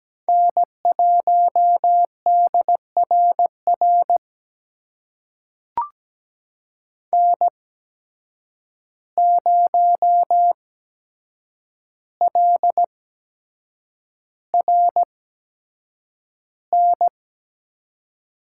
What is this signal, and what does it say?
Morse code, heard as a keyed tone, about 17 wpm, spaced as for 7 wpm, 700 Hz. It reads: N1DRR N0LRN
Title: USA Call Signs - 5 Characters - 8x Character Spacing - 17wpm